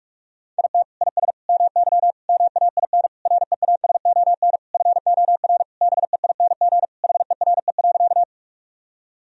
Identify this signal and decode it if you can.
Morse code, heard as a keyed tone, about 46 wpm, 700 Hz. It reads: IT IS MY MAIN REASON FOR BEING HERE.
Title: Sentences from Top 1000 Words - Review Part 2 - 45wpm